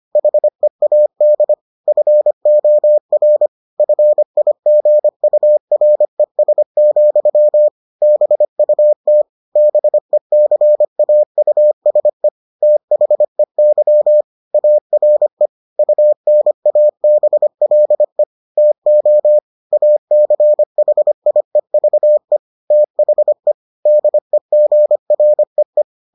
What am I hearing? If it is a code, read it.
Morse code, heard as a keyed tone, about 25 wpm, 600 Hz. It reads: HEAD FOR FIGURES, BUT BECAUSE THEY ARE UNABLE TO ACHIEVE THE DEGREE